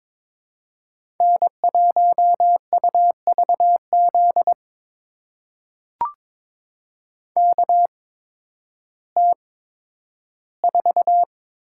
Morse code, heard as a keyed tone, about 22 wpm, spaced as for 12 wpm, 700 Hz. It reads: N1UVZ KT4
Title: USA Call Signs - 5 Characters - 8x Character Spacing - 22wpm